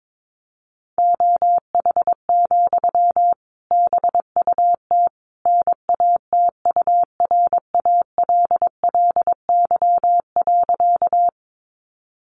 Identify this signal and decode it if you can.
Morse code, heard as a keyed tone, about 22 wpm, 700 Hz. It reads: OH, BUT NATURALLY.
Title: Sentences from Top 1300 Words 22wpm